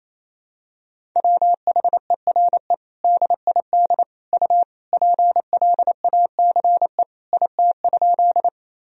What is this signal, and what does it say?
Morse code, heard as a keyed tone, about 28 wpm, 700 Hz. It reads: WHERE DID U PLACE IT?
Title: Sentences from Top 200 Words and Abbreviations 28wpm